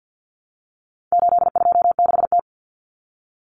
Morse code, heard as a keyed tone, about 50 wpm, 700 Hz. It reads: 826T
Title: Common Cut Numbers - 4-Digit 50wpm